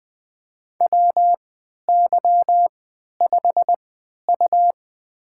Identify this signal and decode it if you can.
Morse code, heard as a keyed tone, about 20 wpm, spaced as for 11 wpm, 700 Hz. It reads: WY5U
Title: USA Call Signs - 3x Character Spacing - 20wpm